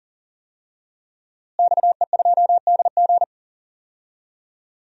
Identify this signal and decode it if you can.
Morse code, heard as a keyed tone, about 39 wpm, 700 Hz. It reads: XE2DG
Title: DXCC-100 Call Signs - 5 Characters - Repeated 2x - 40wpm